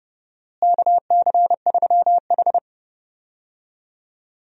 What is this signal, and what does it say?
Morse code, heard as a keyed tone, about 30 wpm, 700 Hz. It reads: KC3H
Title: USA Call Signs - 4 Characters - Repeated 2x - 30wpm